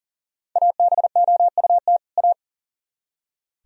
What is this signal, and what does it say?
Morse code, heard as a keyed tone, about 52 wpm, 690 Hz. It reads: ABOUT A